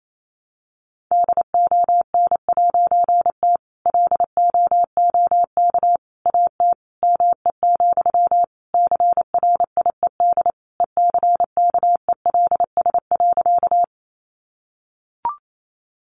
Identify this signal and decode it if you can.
Morse code, heard as a keyed tone, about 28 wpm, 700 Hz. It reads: DON'T LOOK AT ME, CRIED ECKELS.
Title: A Sound Of Thunder 28wpm